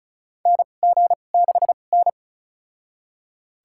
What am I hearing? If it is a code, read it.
Morse code, heard as a keyed tone, about 36 wpm, spaced as for 25 wpm, 700 Hz. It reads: NG6N
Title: USA Call Signs - 4 Characters - 2x Character Spacing - 35wpm